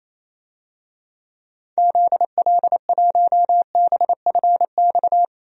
Morse code, heard as a keyed tone, about 27 wpm, 700 Hz. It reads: ZL1BFX
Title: DXCC-100 Call Signs - 6 Characters - 28wpm